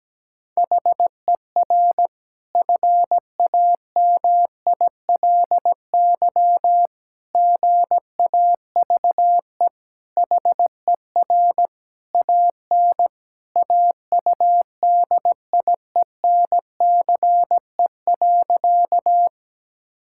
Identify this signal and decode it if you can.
Morse code, heard as a keyed tone, about 17 wpm, 700 Hz. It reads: HER FAMILY GAVE HER AN AUDIENCE.
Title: Sentences from Top 1000 Words - Review Part 5 - 17wpm